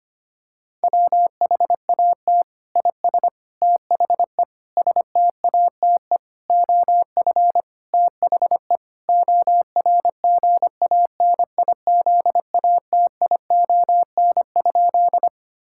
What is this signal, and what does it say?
Morse code, heard as a keyed tone, about 25 wpm, 700 Hz. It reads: WHAT IS THE STATE OF THE ORGANIZATION?
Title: Sentences from Top 1000 Words - Review Part 5 - 25wpm